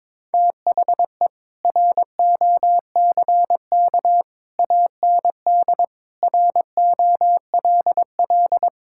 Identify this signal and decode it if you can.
Morse code, heard as a keyed tone, about 22 wpm, 700 Hz. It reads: THE ROCK AND ROLL
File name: practice / 22wpm / Sets of 7 Words - 22wpm